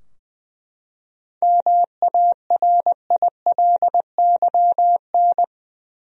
Morse code, heard as a keyed tone, about 20 wpm, 700 Hz. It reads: MARILYN
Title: USA Common Names - 20wpm